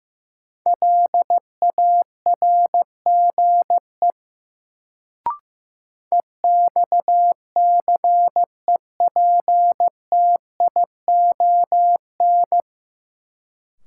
Morse code, heard as a keyed tone, about 15 wpm, 700 Hz. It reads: LARGE EXCEPTION
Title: Sets of 4 Words - New Word Drill - 15wpm